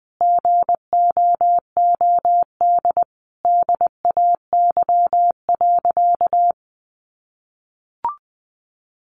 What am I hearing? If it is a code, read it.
Morse code, heard as a keyed tone, about 20 wpm, 700 Hz. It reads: GOOD DAY.
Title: Sentences from Top 600 Words 20wpm